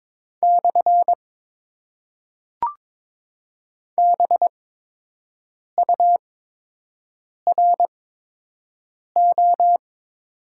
Morse code, heard as a keyed tone, about 22 wpm, spaced as for 6 wpm, 700 Hz. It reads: / BURO
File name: Top 90 QSO Elements - 8x Character Spacing - 22wpm